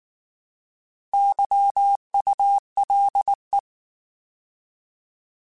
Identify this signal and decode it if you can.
Morse code, heard as a keyed tone, about 19 wpm, 800 Hz. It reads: YULE
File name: Alphabetical / Alphabetical mp3s / Ys @18